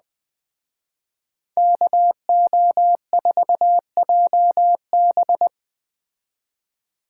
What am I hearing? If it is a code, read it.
Morse code, heard as a keyed tone, about 19 wpm, 700 Hz. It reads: KO4JB
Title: Speed Racing - USA 2x2 Call Signs 17wpm